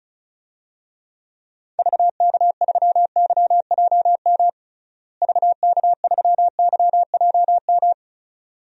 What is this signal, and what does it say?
Morse code, heard as a keyed tone, about 34 wpm, 700 Hz. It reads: VK3YJM VK3YJM
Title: DXCC-100 Call Signs - 6 Characters - Repeated 2x - 35wpm